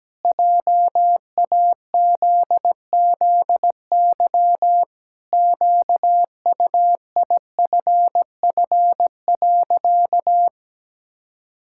Morse code, heard as a keyed tone, about 17 wpm, 700 Hz. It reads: JAZZY QUIFF.
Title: Pangram Challenge v2 - 17wpm